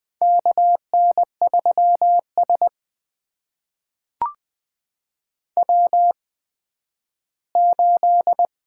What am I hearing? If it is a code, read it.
Morse code, heard as a keyed tone, about 20 wpm, spaced as for 12 wpm, 700 Hz. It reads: KN3S W8
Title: USA Call Signs - 4 Characters - 8x Character Spacing - 20wpm